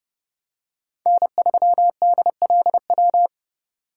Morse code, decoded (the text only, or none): N3DLW